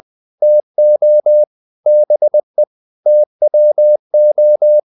Morse code, heard as a keyed tone, about 20 wpm, 600 Hz. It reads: TO BE TWO
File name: the old man and the sea part 2 20wpm